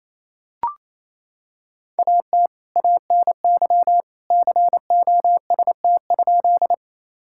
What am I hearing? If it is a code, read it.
Morse code, heard as a keyed tone, about 28 wpm, 700 Hz. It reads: AT ANY COST?
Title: Sentences from Top 1000 Words - Review Part 5 - 28wpm